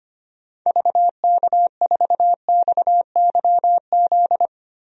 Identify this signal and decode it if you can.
Morse code, heard as a keyed tone, about 25 wpm, 700 Hz. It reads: VK4XYZ
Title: DXCC-100 Call Signs - 6 Characters - Repeated 3x - 25wpm